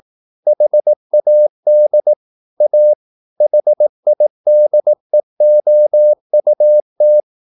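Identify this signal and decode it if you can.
Morse code, heard as a keyed tone, about 18 wpm, 600 Hz. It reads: HAD A HIDEOUT
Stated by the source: Morse practice at 18 wpm, English text, from CW med Gnister.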